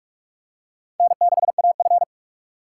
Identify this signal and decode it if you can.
Morse code, heard as a keyed tone, about 44 wpm, 700 Hz. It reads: N6AF